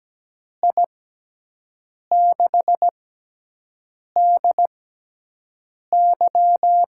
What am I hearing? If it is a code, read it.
Morse code, heard as a keyed tone, about 17 wpm, spaced as for 6 wpm, 700 Hz. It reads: I6DY
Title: USA Call Signs - 6 Characters - 6x Character Spacing - 17wpm